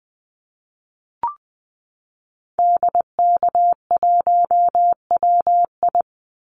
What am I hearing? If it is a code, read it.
Morse code, heard as a keyed tone, about 20 wpm, 700 Hz. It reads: DK1WI
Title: Call Signs - 20wpm